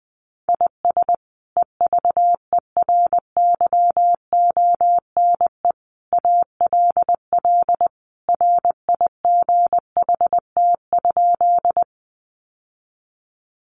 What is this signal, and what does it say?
Morse code, heard as a keyed tone, about 20 wpm, 700 Hz. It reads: IS EVERYONE ALL RIGHT?
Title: Sentences from Top 800 Words 20wpm